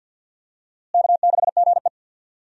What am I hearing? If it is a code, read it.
Morse code, heard as a keyed tone, about 50 wpm, 700 Hz. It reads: K6GE
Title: USA Call Signs - 4 Characters - Repeated 3x - 50wpm